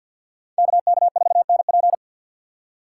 Morse code, heard as a keyed tone, about 63 wpm, 700 Hz. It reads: KK4NP